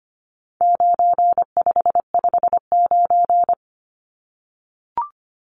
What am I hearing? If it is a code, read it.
Morse code, heard as a keyed tone, about 25 wpm, 700 Hz. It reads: 9559
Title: Numbers - 4-Digit 25wpm